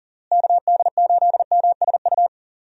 Morse code, heard as a keyed tone, about 40 wpm, 700 Hz. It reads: KD8MSU
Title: USA Call Signs - 6 Characters - Repeated 3x - 40wpm